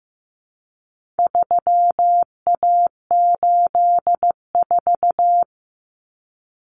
Morse code, heard as a keyed tone, about 15 wpm, 700 Hz. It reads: 3A84
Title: Common Cut Numbers - 4-Digit 15wpm